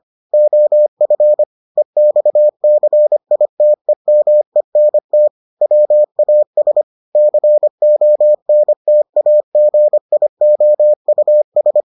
Morse code, heard as a keyed tone, about 25 wpm, 600 Hz. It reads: OF EXCITEMENT WAS CONTAGIOUS